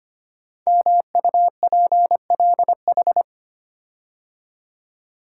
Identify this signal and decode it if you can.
Morse code, heard as a keyed tone, about 25 wpm, 700 Hz. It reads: MUPLH
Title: Code Groups - 5 Letters - 25wpm